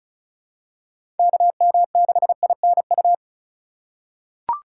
Morse code, heard as a keyed tone, about 36 wpm, 700 Hz. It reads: KM6INU
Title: USA Call Signs - 6 Characters - 3x Character Spacing - 35wpm